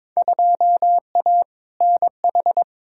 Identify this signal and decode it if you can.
Morse code, heard as a keyed tone, about 22 wpm, 700 Hz. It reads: 2A NH
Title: Field Day Exchanges Encore - 22wpm